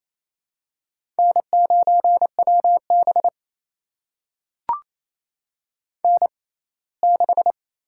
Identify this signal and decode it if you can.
Morse code, heard as a keyed tone, about 28 wpm, spaced as for 19 wpm, 700 Hz. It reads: N9WB N6
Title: USA Call Signs - 4 Characters - 6x Character Spacing - 28wpm